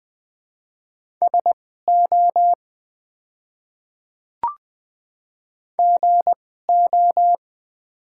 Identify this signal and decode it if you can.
Morse code, heard as a keyed tone, about 20 wpm, spaced as for 15 wpm, 700 Hz. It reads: SO GO
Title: 2x Character Spacing - Top 100 Words - 20wpm